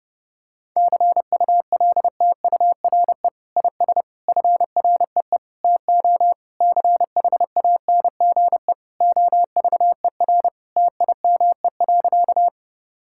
Morse code, heard as a keyed tone, about 30 wpm, 700 Hz. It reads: CULTURE IS FREE TO CHANGE OVER TIME.
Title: Sentences from Top 1000 Words - Review Part 6 - 30wpm